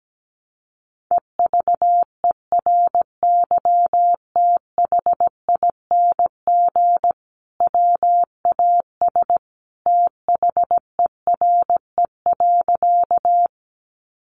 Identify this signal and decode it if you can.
Morse code, heard as a keyed tone, about 17 wpm, 700 Hz. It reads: EVERYTHING WAS THERE.
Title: Sentences from Top 700 Words 17wpm